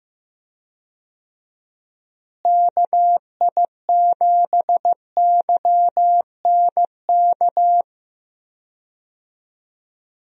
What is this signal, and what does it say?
Morse code, heard as a keyed tone, about 15 wpm, 700 Hz. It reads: KI7YNK
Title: USA Call Signs - 6 Characters - Repeated 3x - 15wpm